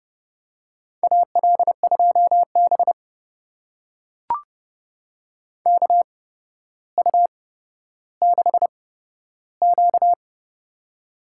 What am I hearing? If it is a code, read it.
Morse code, heard as a keyed tone, about 32 wpm, spaced as for 13 wpm, 700 Hz. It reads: AL2B KU6Q
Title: USA Call Signs - 4 Characters - 8x Character Spacing - 30wpm